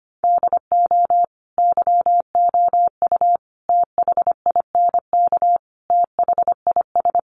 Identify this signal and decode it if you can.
Morse code, heard as a keyed tone, about 25 wpm, 700 Hz. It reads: DO YOU THINK THIS
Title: Sets of 5 Words Encore2 25wpm